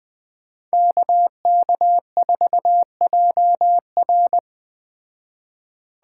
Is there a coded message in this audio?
Yes: Morse code from 0.7 to 4.4 s, heard as a keyed tone, about 21 wpm, 700 Hz. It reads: KK4JR